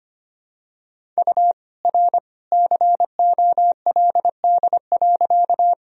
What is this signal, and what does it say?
Morse code, heard as a keyed tone, about 25 wpm, 700 Hz. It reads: U R COLD.
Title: Sentences from Top 400 Words and Abbreviations 25wpm